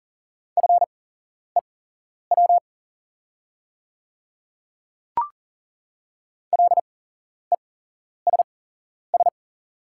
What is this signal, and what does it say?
Morse code, heard as a keyed tone, about 40 wpm, spaced as for 11 wpm, 690 Hz. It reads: FEW LESS
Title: 8x Character Spacing - Top 500 Words - 40wpm